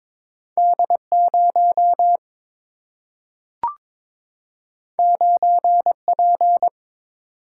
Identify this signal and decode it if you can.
Morse code, heard as a keyed tone, about 22 wpm, 700 Hz. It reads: D 0 9 P
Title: Single-Letter Single-Number 22wpm